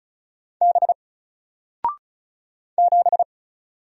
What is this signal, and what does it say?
Morse code, heard as a keyed tone, about 36 wpm, 700 Hz. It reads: B 7
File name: Single-Letter-Number-35wpm